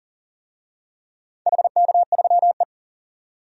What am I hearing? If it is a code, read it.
Morse code, heard as a keyed tone, about 40 wpm, 690 Hz. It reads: HK3E
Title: DXCC-100 Call Signs - 4 Characters - 40wpm